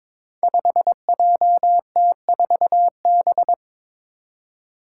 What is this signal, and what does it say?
Morse code, heard as a keyed tone, about 22 wpm, 700 Hz. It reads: 5JT4B
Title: Five Challenging Letter-Number Combinations 22wpm